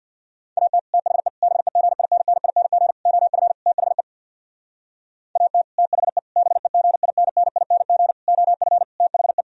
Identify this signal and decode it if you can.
Morse code, heard as a keyed tone, about 65 wpm, 700 Hz. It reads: AT THE BEGINNING OF THE AT THE BEGINNING OF THE